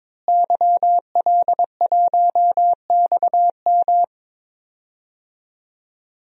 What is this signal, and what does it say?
Morse code, heard as a keyed tone, about 22 wpm, 700 Hz. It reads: YL1XM
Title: DXCC-100 Call Signs - 5 Characters - Repeated 2x - 22wpm